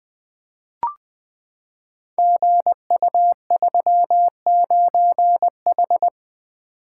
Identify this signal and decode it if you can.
Morse code, heard as a keyed tone, about 20 wpm, 700 Hz. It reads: GU39H
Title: Code Groups - 5 Letter:Numbers - 20wpm